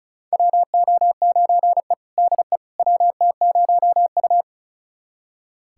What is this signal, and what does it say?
Morse code, heard as a keyed tone, about 36 wpm, 700 Hz. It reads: WO9E DE WT0U
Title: USA Callsign DE Callsign - 4 Characters - 35wpm